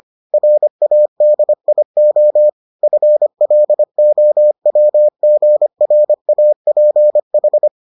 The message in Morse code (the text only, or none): RADIO FLOWGRAPH